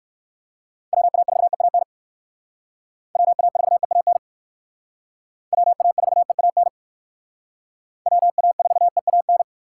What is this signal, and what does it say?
Morse code, heard as a keyed tone, about 68 wpm, 700 Hz. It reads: WA4EAN WA4EAN WA4EAN WA4EAN